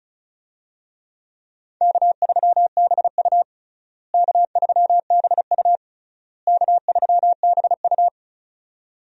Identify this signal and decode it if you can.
Morse code, heard as a keyed tone, about 36 wpm, 700 Hz. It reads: K3BU K3BU K3BU
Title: USA Call Signs - 4 Characters - Repeated 3x - 35wpm